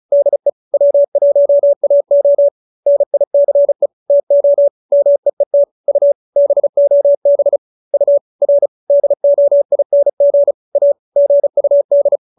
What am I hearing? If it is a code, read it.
Morse code, heard as a keyed tone, about 34 wpm, 570 Hz. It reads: DE W1AO NICE TO MEET U BOB U R DOING A GUD
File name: QSO313 35